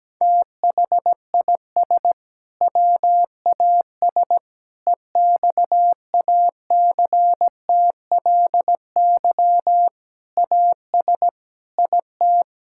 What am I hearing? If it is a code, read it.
Morse code, heard as a keyed tone, about 17 wpm, 700 Hz. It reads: THIS WAS EXACTLY AS IT
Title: Sentences from Top 1000 Words - Review Part 6 - 17wpm